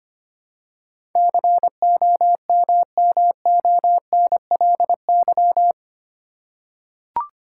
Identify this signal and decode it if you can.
Morse code, heard as a keyed tone, about 25 wpm, 700 Hz. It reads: COMMONLY
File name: 8 Letter Words - First 3 Letters Given - 25wpm